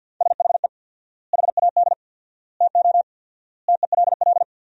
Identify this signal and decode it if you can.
Morse code, heard as a keyed tone, about 50 wpm, 700 Hz. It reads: SHE HAD TO TELL
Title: 4x Word-Spacing - Sentences from Top 200 Words - 50wpm